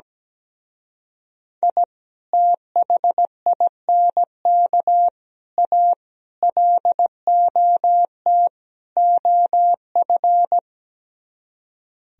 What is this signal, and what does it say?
Morse code, heard as a keyed tone, about 17 wpm, 700 Hz. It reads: I THINK A LOT OF
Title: Speed-Racing - Sets of 5 Words 17wpm